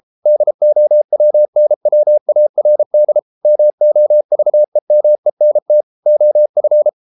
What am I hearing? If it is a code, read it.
Morse code, heard as a keyed tone, about 33 wpm, 600 Hz. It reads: DOWNWARD MOVEMENT OF